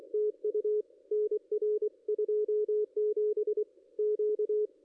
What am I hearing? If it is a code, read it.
Morse code, heard as a keyed tone, about 24 wpm, 420 Hz, with radio noise. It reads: TU NR 27 Q
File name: contests / ss-cw-k5tr-op-n5rz -2200